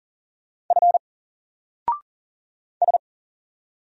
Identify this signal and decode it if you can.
Morse code, heard as a keyed tone, about 40 wpm, 700 Hz. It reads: F S
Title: Single Character (Letter:Number:Punctuation:Prosign) Alt 40wpm